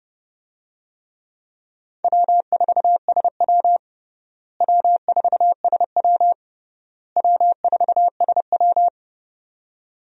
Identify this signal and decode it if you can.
Morse code, heard as a keyed tone, about 30 wpm, 700 Hz. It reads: W4SW W4SW W4SW
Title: USA Call Signs - 4 Characters - Repeated 3x - 30wpm